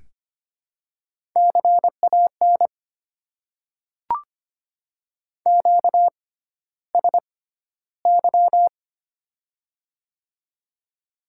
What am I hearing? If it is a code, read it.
Morse code, heard as a keyed tone, about 26 wpm, spaced as for 13 wpm, 700 Hz. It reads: CAN QSY
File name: Top 90 QSO Elements - 6x Character Spacing - 25wpm